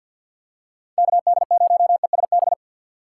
Morse code, heard as a keyed tone, about 52 wpm, 700 Hz. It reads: KD0ESB
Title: USA Call Signs - 2x Character Spacing - 50wpm